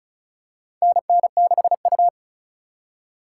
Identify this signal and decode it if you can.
Morse code, heard as a keyed tone, about 34 wpm, 700 Hz. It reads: NN6U
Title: USA Call Signs - 2x Character Spacing - 35wpm